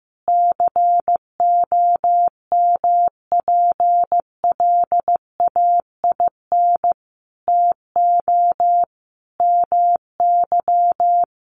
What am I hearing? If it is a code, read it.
Morse code, heard as a keyed tone, about 15 wpm, 700 Hz. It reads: COMPLAIN TO MY
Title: Sentences from Top 1300 Words 15wpm